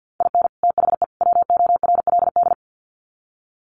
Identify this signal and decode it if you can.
Morse code, heard as a keyed tone, about 50 wpm, 700 Hz. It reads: IN THE WORLD
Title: Sets of 3 Words 50wpm